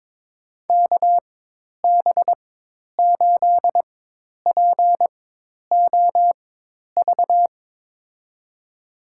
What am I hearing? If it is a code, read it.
Morse code, heard as a keyed tone, about 21 wpm, spaced as for 10 wpm, 700 Hz. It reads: KB8POV
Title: USA Call Signs - 6 Characters - 4x Character Spacing - 22wpm